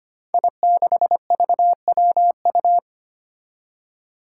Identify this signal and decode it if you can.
Morse code, heard as a keyed tone, about 25 wpm, 700 Hz. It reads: I6VWU
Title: Five Challenging Letter-Number Combinations 25wpm